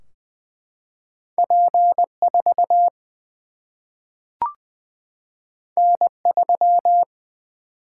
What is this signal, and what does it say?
Morse code, heard as a keyed tone, about 20 wpm, 700 Hz. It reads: P 4 N 3
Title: Single-Letter Single-Number 20wpm